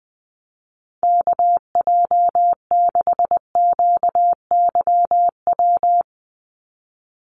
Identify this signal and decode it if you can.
Morse code, heard as a keyed tone, about 20 wpm, 700 Hz. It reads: KJ6QYW
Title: USA Call Signs 20wpm